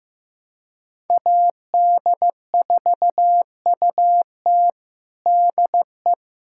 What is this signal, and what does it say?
Morse code, heard as a keyed tone, about 16 wpm, 700 Hz. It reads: AD4UT DE
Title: USA Callsign DE Callsign - 5 Characters - 15wpm